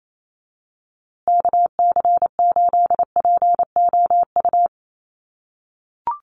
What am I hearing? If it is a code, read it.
Morse code, heard as a keyed tone, about 28 wpm, 700 Hz. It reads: KC8POU
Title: USA Call Signs Repeated 3x 28wpm